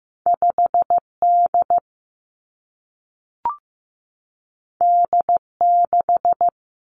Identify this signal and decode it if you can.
Morse code, heard as a keyed tone, about 15 wpm, 700 Hz. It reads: 5 D D 6
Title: Two Challenging Letter-Number Combinations 15wpm